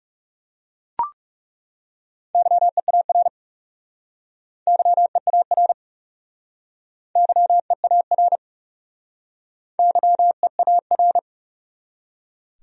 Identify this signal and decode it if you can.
Morse code, heard as a keyed tone, about 41 wpm, 700 Hz. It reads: YEAR YEAR YEAR YEAR